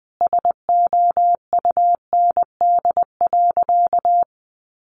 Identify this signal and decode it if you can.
Morse code, heard as a keyed tone, about 20 wpm, 700 Hz. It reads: SOUND.
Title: Sentences from Top 900 Words 20wpm